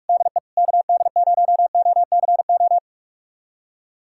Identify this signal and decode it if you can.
Morse code, heard as a keyed tone, about 47 wpm, 700 Hz. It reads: DE KD0OCO
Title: USA Callsign DE Callsign - 6 Characters - 45wpm